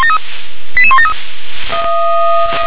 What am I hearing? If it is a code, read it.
Five-tone selective call (ZVEI), heard as quick runs of five tones, tones of about 70 ms. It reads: 88173